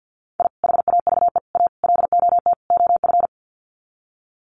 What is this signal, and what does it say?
Morse code, heard as a keyed tone, about 50 wpm, 700 Hz. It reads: I HAVE A LOT OF
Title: Sets of 5 Words 50wpm